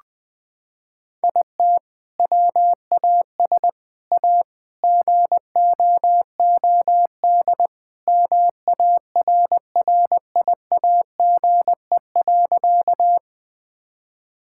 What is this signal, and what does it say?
Morse code, heard as a keyed tone, about 21 wpm, 700 Hz. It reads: IT WAS A GOOD MARRIAGE.
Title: Sentences from Top 1000 Words - Review Part 5 - 20wpm